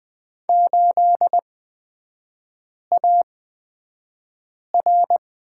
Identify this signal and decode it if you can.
Morse code, heard as a keyed tone, about 20 wpm, spaced as for 5 wpm, 700 Hz. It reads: 8AR
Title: 224 Lesson 36e - BT - Three Characters 20wpm